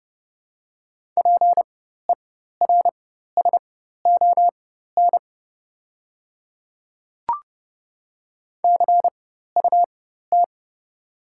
Morse code, heard as a keyed tone, about 30 wpm, spaced as for 14 wpm, 700 Hz. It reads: PERSON CUT
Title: 4x Character Spacing - Top 500 Words - 30wpm